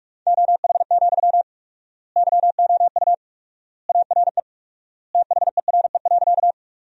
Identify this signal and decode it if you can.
Morse code, heard as a keyed tone, about 47 wpm, 700 Hz. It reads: OH, YOU ARE THERE.
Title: 4x Word-Spacing - Sentences from Top 500 Words - 45wpm